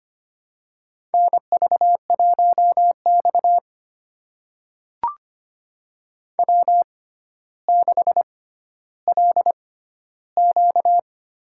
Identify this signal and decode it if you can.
Morse code, heard as a keyed tone, about 25 wpm, spaced as for 13 wpm, 700 Hz. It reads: NV1X W6LQ
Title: USA Call Signs - 4 Characters - 6x Character Spacing - 25wpm